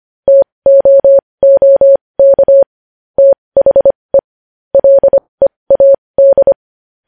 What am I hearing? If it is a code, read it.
Morse code, heard as a keyed tone, about 25 wpm, spaced as for 20 wpm, 550 Hz. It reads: TOOK THE LEAD
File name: podcast / News headlines-1x-25x20